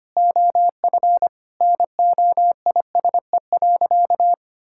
Morse code, heard as a keyed tone, about 25 wpm, 700 Hz. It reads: OF NOISE.
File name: practice / 25wpm / Sentences with 4-Word Phrases - 25wpm